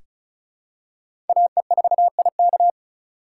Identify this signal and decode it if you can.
Morse code, heard as a keyed tone, about 34 wpm, 700 Hz. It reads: AE4IK
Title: USA Call Signs - 5 Characters - 2x Character Spacing - 35wpm